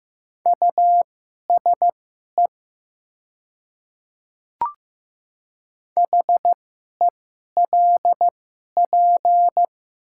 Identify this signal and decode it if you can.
Morse code, heard as a keyed tone, about 15 wpm, spaced as for 11 wpm, 700 Hz. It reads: USE HELP
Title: 2x Character Spacing - Top 200 Words - 15wpm